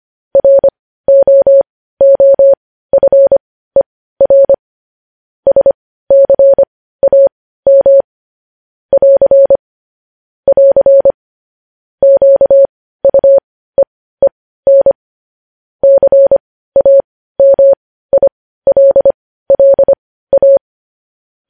Morse code, heard as a keyed tone, about 25 wpm, spaced as for 15 wpm, 550 Hz. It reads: ROOFER SCAM <AR> <AR> QUEEN CAMILLA
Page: news headlines in morse code at 15 wpm